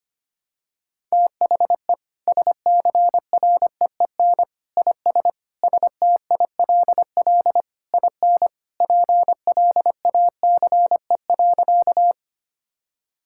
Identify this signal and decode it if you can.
Morse code, heard as a keyed tone, about 25 wpm, 700 Hz. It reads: THE SCREEN IS STILL IN PLACE.
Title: Sentences from Top 1000 Words - Review Part 6 - 25wpm